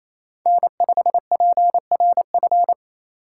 Morse code, heard as a keyed tone, about 33 wpm, 700 Hz. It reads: N5PRF
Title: Speed Racing - USA 1x3 Call Signs 28wpm